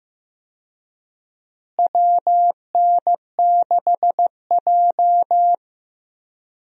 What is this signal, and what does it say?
Morse code, heard as a keyed tone, about 16 wpm, 700 Hz. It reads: WN6J
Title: USA Call Signs - 4 Characters - Repeated 2x - 15wpm